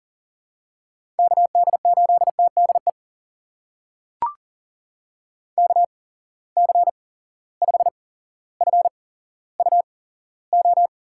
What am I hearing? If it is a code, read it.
Morse code, heard as a keyed tone, about 40 wpm, spaced as for 17 wpm, 700 Hz. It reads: KD8TDE KC5FUO
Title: USA Call Signs - 6 Characters - 8x Character Spacing - 40wpm